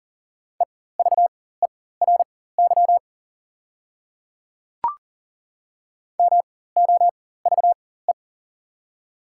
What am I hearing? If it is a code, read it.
Morse code, heard as a keyed tone, about 37 wpm, spaced as for 18 wpm, 700 Hz. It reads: EVERY MOVE
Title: 4x Character Spacing - Top 300 Words - 40wpm